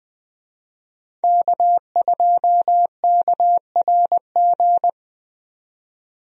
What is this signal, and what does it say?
Morse code, heard as a keyed tone, about 20 wpm, 700 Hz. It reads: K2KRG
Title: USA Call Signs - 5 Characters - 20wpm